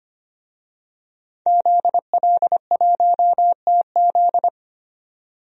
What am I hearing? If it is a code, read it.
Morse code, heard as a keyed tone, about 25 wpm, 700 Hz. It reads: ZL1TZ
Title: DXCC-100 Call Signs - 5 Characters - 25wpm